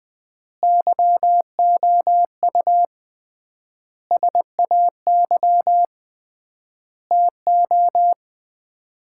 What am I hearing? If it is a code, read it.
Morse code, heard as a keyed tone, about 20 wpm, 700 Hz. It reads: YOU SAY TO
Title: 3x Word-Spacing - Sets of 5 Words - 20wpm